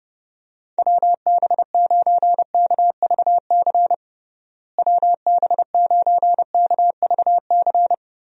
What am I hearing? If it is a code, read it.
Morse code, heard as a keyed tone, about 32 wpm, 700 Hz. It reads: WB9KVC WB9KVC
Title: USA Call Signs - 6 Characters - Repeated 3x - 30wpm